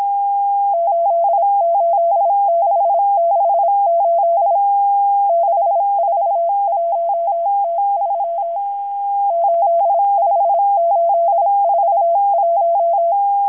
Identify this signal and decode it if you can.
Morse code, heard as a keyed tone, about 26 wpm, 680 Hz. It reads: 88668 641T3 85841
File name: HM02- -0437utc-7351khz